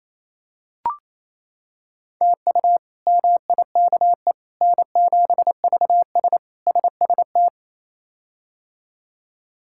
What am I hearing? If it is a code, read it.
Morse code, heard as a keyed tone, about 28 wpm, 700 Hz. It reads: TU MIKE N7VS SST
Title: K1USN SST QSO Practice - 28wpm